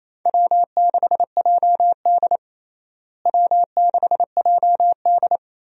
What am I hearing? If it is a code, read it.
Morse code, heard as a keyed tone, about 28 wpm, 700 Hz. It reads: W6JD W6JD
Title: USA Call Signs - 4 Characters - Repeated 3x - 28wpm